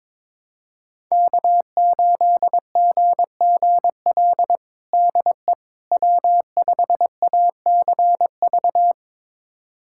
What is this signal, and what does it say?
Morse code, heard as a keyed tone, about 22 wpm, 700 Hz. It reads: K8GGL DE W5ACV
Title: USA Callsign DE Callsign - 5 Characters - 22wpm